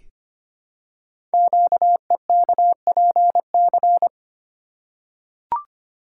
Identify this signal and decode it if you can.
Morse code, heard as a keyed tone, about 25 wpm, 700 Hz. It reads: QEKPC
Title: Code Groups - 5 Letters - 25wpm